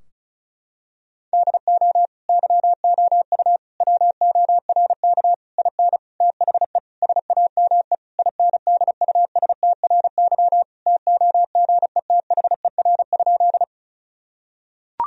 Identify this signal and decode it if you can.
Morse code, heard as a keyed tone, about 36 wpm, 700 Hz. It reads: DO YOU WORK IN THE SAME INDUSTRY TOGETHER?
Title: Sentences from Top 1000 Words - Review Part 4 - 35wpm